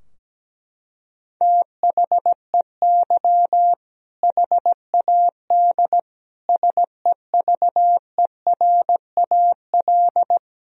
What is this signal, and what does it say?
Morse code, heard as a keyed tone, about 17 wpm, 700 Hz. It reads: THEY HAD SEVERAL
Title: Sentences from Top 1000 Words - Review Part 2 - 17wpm